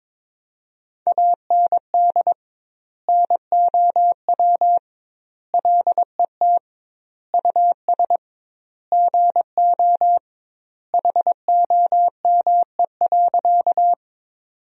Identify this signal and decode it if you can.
Morse code, heard as a keyed tone, about 22 wpm, 700 Hz. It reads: AND NOW LET US GO HOME.
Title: 2x Word-Spacing - Sentences from Top 300 Words - 22wpm